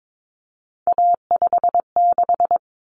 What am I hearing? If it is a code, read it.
Morse code, heard as a keyed tone, about 22 wpm, 700 Hz. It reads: A56
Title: Common Cut Numbers - 3-Digit 22wpm